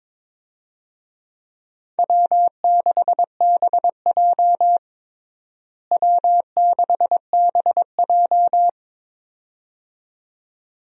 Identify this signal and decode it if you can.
Morse code, heard as a keyed tone, about 21 wpm, 700 Hz. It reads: W6BJ W6BJ
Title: USA Call Signs - 4 Characters - Repeated 2x - 22wpm